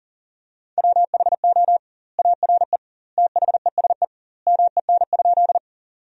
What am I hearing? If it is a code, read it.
Morse code, heard as a keyed tone, about 40 wpm, 700 Hz. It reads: WHO ARE THESE MEN?
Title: 2x Word-Spacing - Sentences from Top 200 Words - 40wpm